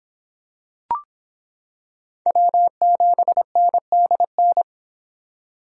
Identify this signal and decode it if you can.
Morse code, heard as a keyed tone, about 26 wpm, 700 Hz. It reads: W7NDN